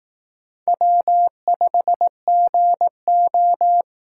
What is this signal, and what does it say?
Morse code, heard as a keyed tone, about 18 wpm, 700 Hz. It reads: W5GO
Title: Speed Racing - USA Call Signs 15wpm